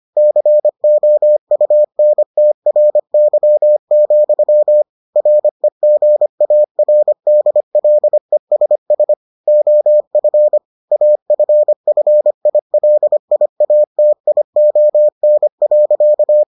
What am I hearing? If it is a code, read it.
Morse code, heard as a keyed tone, about 25 wpm, 600 Hz. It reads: COUNTRY, REGARDLESS OF AFFILIATION.